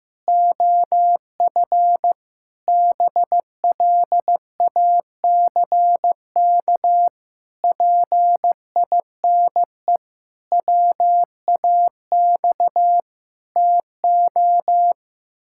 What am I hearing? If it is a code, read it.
Morse code, heard as a keyed tone, about 14 wpm, 700 Hz. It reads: OF BLACK PINE WAX TO
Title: Pangram Challenge v2 - 15wpm